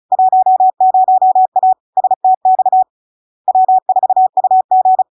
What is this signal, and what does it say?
Morse code, heard as a keyed tone, about 36 wpm, 750 Hz. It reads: 10A STX W4UG
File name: LARK-FD-2022-Log-in-CW-35wpm-750-Hz